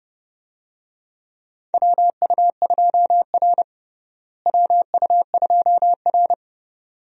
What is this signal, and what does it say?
Morse code, heard as a keyed tone, about 31 wpm, 700 Hz. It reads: WU2R WU2R
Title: USA Call Signs - 4 Characters - Repeated 2x - 30wpm